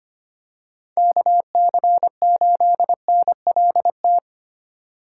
Morse code, heard as a keyed tone, about 25 wpm, 700 Hz. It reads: KC8NLT
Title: USA Call Signs - 6 Characters - 2x Character Spacing - 25wpm